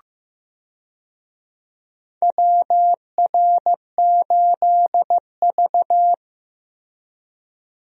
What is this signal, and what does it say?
Morse code, heard as a keyed tone, about 15 wpm, 700 Hz. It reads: WR8V
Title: USA Call Signs - 4 Characters - 15wpm